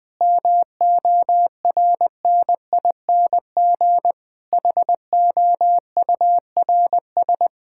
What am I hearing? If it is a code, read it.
Morse code, heard as a keyed tone, about 20 wpm, 700 Hz. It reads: MORNING HOURS
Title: Sentences with 6-Word Phrases - Spoken Beginning - 20wpm